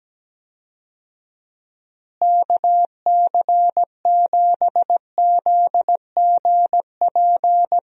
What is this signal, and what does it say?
Morse code, heard as a keyed tone, about 17 wpm, 700 Hz. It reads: KC7ZGP
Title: USA Call Signs - 6 Characters - Repeated 2x - 17wpm